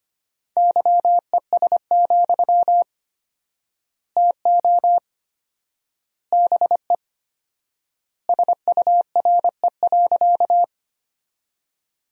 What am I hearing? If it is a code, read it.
Morse code, heard as a keyed tone, about 25 wpm, 700 Hz. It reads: YES, TO BE SURE.
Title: 4x Word-Spacing - Sentences from Top 500 Words - 25wpm